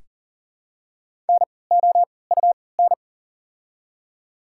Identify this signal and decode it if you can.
Morse code, heard as a keyed tone, about 40 wpm, spaced as for 23 wpm, 700 Hz. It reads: NOUN
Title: 3x Character Spacing - Top 500 Words - 40wpm